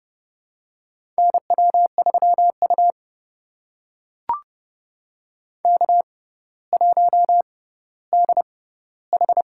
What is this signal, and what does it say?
Morse code, heard as a keyed tone, about 30 wpm, spaced as for 15 wpm, 700 Hz. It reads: NW3U K1DH
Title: USA Call Signs - 4 Characters - 6x Character Spacing - 30wpm